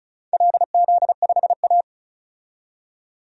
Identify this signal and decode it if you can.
Morse code, heard as a keyed tone, about 36 wpm, 700 Hz. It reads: LZ5A
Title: DXCC-100 Call Signs - 4 Characters - 35wpm